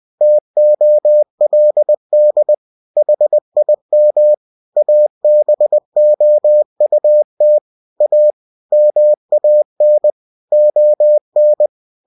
Morse code, heard as a keyed tone, about 20 wpm, 600 Hz. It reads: TOLD HIM ABOUT A MAN ON
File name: cw-books / winnie-the-pooh / winnie-20wpm-page-0092